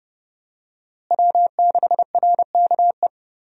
Morse code, heard as a keyed tone, about 32 wpm, 700 Hz. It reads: W6RKE
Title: USA Call Signs - 5 Characters - Repeated 3x - 30wpm